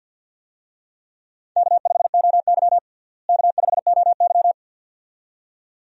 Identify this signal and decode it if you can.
Morse code, heard as a keyed tone, about 52 wpm, 700 Hz. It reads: K5OY K5OY
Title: USA Call Signs - 4 Characters - Repeated 2x - 50wpm